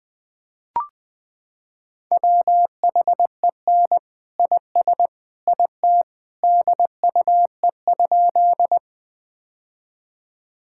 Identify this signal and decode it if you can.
Morse code, heard as a keyed tone, about 21 wpm, 700 Hz. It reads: WHEN IS IT DUE?
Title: Sentences from Top 1000 Words - Review Part 6 - 20wpm